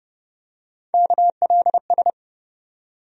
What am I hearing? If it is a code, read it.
Morse code, heard as a keyed tone, about 30 wpm, 700 Hz. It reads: KLS
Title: USA Call Sign Postfixes - 3 Letters 30wpm